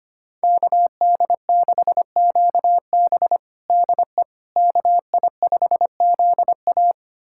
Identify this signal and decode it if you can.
Morse code, heard as a keyed tone, about 25 wpm, 700 Hz. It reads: KD6QB DE KI5ZA
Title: USA Callsign DE Callsign - 5 Characters - 25wpm